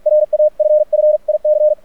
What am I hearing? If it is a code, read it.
Morse code, heard as a keyed tone, about 46 wpm, 610 Hz. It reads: RAFFIC
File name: archived / CRSnet